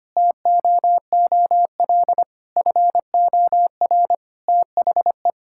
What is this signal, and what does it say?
Morse code, heard as a keyed tone, about 25 wpm, 700 Hz. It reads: TOOL FOR THE
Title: Sentences from Top 1000 Words - Review Part 6 - 25wpm